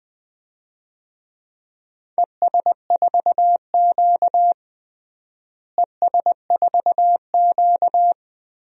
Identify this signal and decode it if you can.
Morse code, heard as a keyed tone, about 21 wpm, 700 Hz. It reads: ES4Q ES4Q
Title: DXCC-100 Call Signs - 4 Characters - Repeated 2x - 20wpm